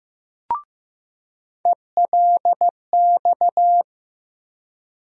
Morse code, heard as a keyed tone, about 15 wpm, 700 Hz. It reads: ELX
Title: USA Call Sign Postfixes - 3 Letters 15wpm